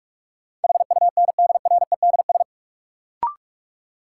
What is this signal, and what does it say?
Morse code, heard as a keyed tone, about 42 wpm, 700 Hz. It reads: HUNDREDS
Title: Sets of 4 Words - New Word Drill - 45wpm